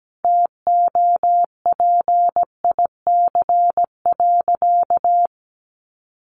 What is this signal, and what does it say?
Morse code, heard as a keyed tone, about 17 wpm, 700 Hz. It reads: TOPIC.
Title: Sentences from Top 1200 Words 17wpm